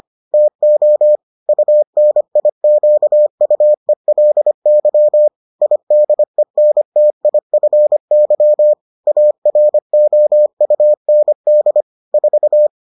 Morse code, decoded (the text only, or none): TO UNIQUELY IDENTIFY AROUND 4